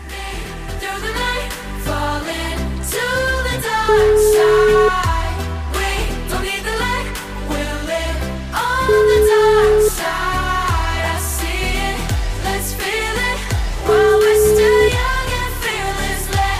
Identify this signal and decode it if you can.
Ringback tone (Europe, 425 Hz), with noise.